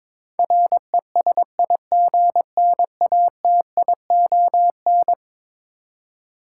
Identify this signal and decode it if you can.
Morse code, heard as a keyed tone, about 21 wpm, 700 Hz. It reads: RESIGNATION
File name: practice / 22wpm / Eleven Letter Words - 22wpm